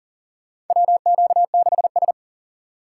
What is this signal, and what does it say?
Morse code, heard as a keyed tone, about 40 wpm, 700 Hz. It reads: WQ6S
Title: USA Call Signs - 2x Character Spacing - 40wpm